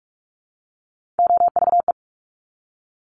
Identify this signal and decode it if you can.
Morse code, heard as a keyed tone, about 46 wpm, 700 Hz. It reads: OVE
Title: Top Three Letter Combinations in Words 45wpm